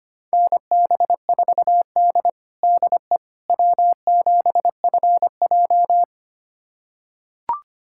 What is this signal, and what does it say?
Morse code, heard as a keyed tone, about 25 wpm, 700 Hz. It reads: NB4D DE W7FJ
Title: USA Callsign DE Callsign - 4 Characters - 25wpm